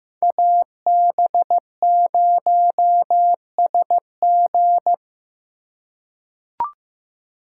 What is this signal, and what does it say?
Morse code, heard as a keyed tone, about 15 wpm, 700 Hz. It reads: AB0SG
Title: USA Call Signs - 2x Character Spacing - 15wpm